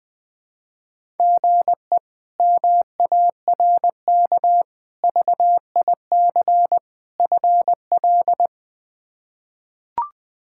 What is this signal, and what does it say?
Morse code, heard as a keyed tone, about 20 wpm, 700 Hz. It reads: GE MARK VIC FL
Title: K1USN SST QSO Practice - 20wpm